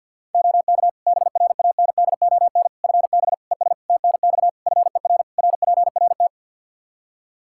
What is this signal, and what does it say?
Morse code, heard as a keyed tone, about 47 wpm, 700 Hz. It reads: OK BRANDON FB ES TNX FER RPRT
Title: Practice QSO - Exchange 2 - Info - 50wpm